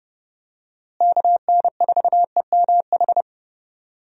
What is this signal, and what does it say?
Morse code, heard as a keyed tone, about 30 wpm, 700 Hz. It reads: KN4EMH
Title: USA Call Signs - 6 Characters - Repeated 3x - 30wpm